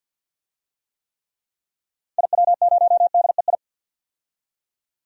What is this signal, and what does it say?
Morse code, heard as a keyed tone, about 52 wpm, 700 Hz. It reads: IW0DEI
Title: DXCC-100 Call Signs - 6 Characters - 50wpm